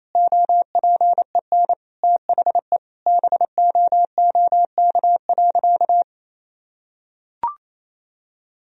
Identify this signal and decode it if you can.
Morse code, heard as a keyed tone, about 28 wpm, 700 Hz. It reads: OPEN THE BOOK.
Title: Sentences from Top 1000 Words - Review Part 1 - 28wpm